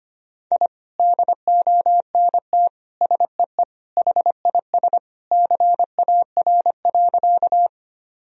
Morse code, heard as a keyed tone, about 25 wpm, 700 Hz. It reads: I DONT SEE HIS CAR.
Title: Sentences from Top 1000 Words - Review Part 1 - 25wpm